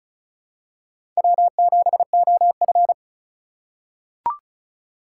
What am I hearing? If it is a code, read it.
Morse code, heard as a keyed tone, about 34 wpm, 700 Hz. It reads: W7OF
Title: USA Call Signs - 4 Characters - 3x Character Spacing - 35wpm